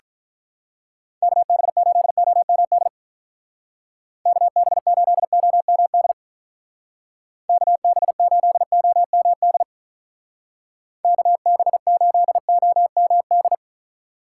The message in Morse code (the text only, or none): KB8OMD KB8OMD KB8OMD KB8OMD